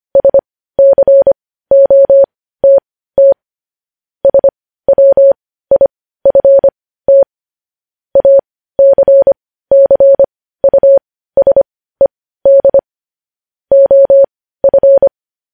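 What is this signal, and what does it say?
Morse code, heard as a keyed tone, about 25 wpm, spaced as for 15 wpm, 550 Hz. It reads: SCOTT SWIFT ACCUSED OF
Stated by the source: Morse code news headlines sent at 15, preceded by vvv.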